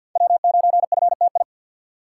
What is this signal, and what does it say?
Morse code, heard as a keyed tone, about 47 wpm, 700 Hz. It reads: W9FTI
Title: USA Call Signs - 3x Character Spacing - 50wpm